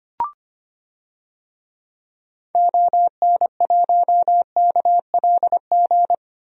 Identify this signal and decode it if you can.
Morse code, heard as a keyed tone, about 25 wpm, 700 Hz. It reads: ON1KLG
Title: DXCC-100 Call Signs - 6 Characters - Repeated 3x - 25wpm